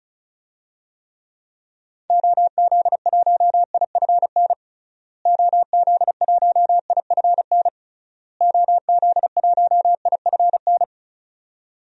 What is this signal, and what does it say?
Morse code, heard as a keyed tone, about 36 wpm, 700 Hz. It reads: OZ1IFN OZ1IFN OZ1IFN
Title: DXCC-100 Call Signs - 6 Characters - Repeated 3x - 35wpm